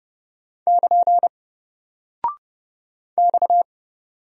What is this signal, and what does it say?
Morse code, heard as a keyed tone, about 30 wpm, 700 Hz. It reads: <KN> X
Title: Single Character (Letter:Number:Punctuation:Prosign) Alt 30wpm